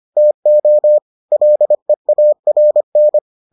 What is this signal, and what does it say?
Morse code, heard as a keyed tone, about 25 wpm, 600 Hz. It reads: TO LEARN